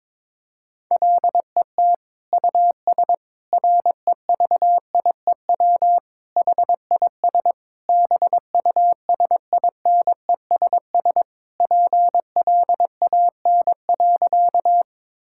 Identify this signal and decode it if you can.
Morse code, heard as a keyed tone, about 22 wpm, 700 Hz. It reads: LET US REVIEW HIS BUSINESS PLAN.
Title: Sentences from Top 1000 Words - Review Part 6 - 22wpm